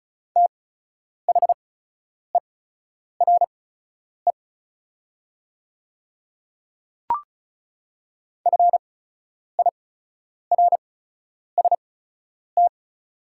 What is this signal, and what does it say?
Morse code, heard as a keyed tone, about 36 wpm, spaced as for 10 wpm, 700 Hz. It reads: THERE FIRST
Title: 8x Character Spacing - Top 100 Words - 35wpm